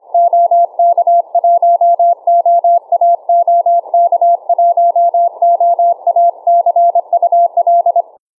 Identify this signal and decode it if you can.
Morse code, heard as a keyed tone, about 26 wpm, 690 Hz. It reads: OK1OA OK1OA CUL